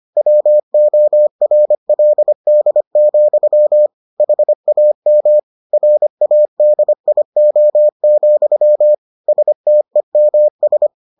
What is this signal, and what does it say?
Morse code, heard as a keyed tone, about 25 wpm, 600 Hz. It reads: WORLD, HAM RADIO, STEMS